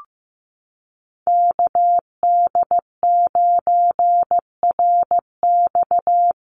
Morse code, heard as a keyed tone, about 15 wpm, 700 Hz. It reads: KD9RX
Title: USA Call Signs Repeated 3x 15wpm